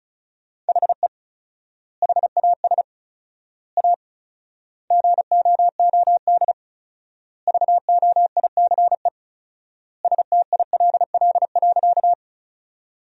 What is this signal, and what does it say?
Morse code, heard as a keyed tone, about 36 wpm, 700 Hz. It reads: HE HAS A GOOD VOICE STILL.
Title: 4x Word-Spacing - Sentences from Top 500 Words - 35wpm